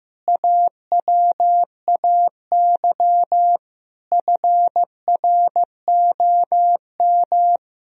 Morse code, heard as a keyed tone, about 15 wpm, 700 Hz. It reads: AWAY FROM
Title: Sentences from Top 1000 Words - Review Part 3 - 15wpm